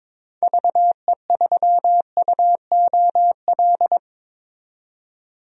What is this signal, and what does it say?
Morse code, heard as a keyed tone, about 22 wpm, 700 Hz. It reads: VE3UOL
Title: DXCC-100 Call Signs - 6 Characters - 22wpm